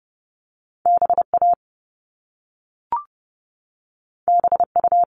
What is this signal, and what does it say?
Morse code, heard as a keyed tone, about 30 wpm, 700 Hz. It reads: BA BU
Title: All Cut Numbers - 2-Digit 30wpm